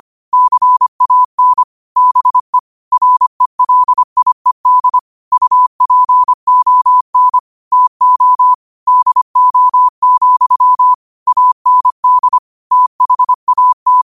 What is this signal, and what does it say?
Morse code, heard as a keyed tone, about 25 wpm, 1000 Hz. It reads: CAN BE RELIED UPON TO DO, AND THAT
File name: qotdfiles / 25wpm